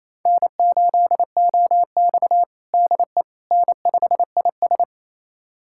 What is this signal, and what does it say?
Morse code, heard as a keyed tone, about 27 wpm, 700 Hz. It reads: N8OX DE N5IS
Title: USA Callsign DE Callsign - 4 Characters - 28wpm